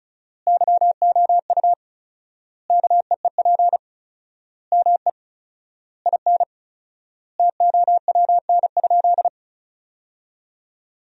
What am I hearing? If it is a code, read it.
Morse code, heard as a keyed tone, about 34 wpm, 700 Hz. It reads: YOU KEEP ME IN TOWN?
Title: 4x Word-Spacing - Sentences from Top 500 Words - 35wpm